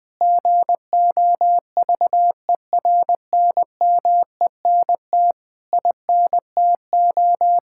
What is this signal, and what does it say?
Morse code, heard as a keyed tone, about 20 wpm, 700 Hz. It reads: GOVERNMENT INTO
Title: Sentences from Top 1000 Words - Review Part 6 - 20wpm